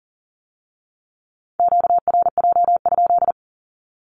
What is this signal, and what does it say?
Morse code, heard as a keyed tone, about 41 wpm, 700 Hz. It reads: QRJ?
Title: Q-Codes with Spoken Meaning - 40wpm